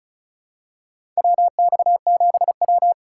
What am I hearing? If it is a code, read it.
Morse code, heard as a keyed tone, about 36 wpm, 700 Hz. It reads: WX7W
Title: USA Call Signs - 4 Characters - 3x Character Spacing - 35wpm